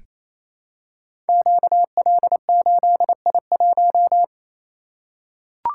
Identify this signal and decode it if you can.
Morse code, heard as a keyed tone, about 28 wpm, 700 Hz. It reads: QL8I1
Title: Code Groups - 5 Letter:Numbers - 28wpm